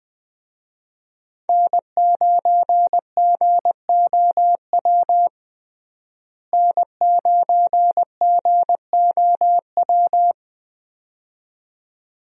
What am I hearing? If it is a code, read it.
Morse code, heard as a keyed tone, about 21 wpm, 700 Hz. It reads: N9GOW N9GOW
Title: USA Call Signs - 5 Characters - Repeated 2x - 20wpm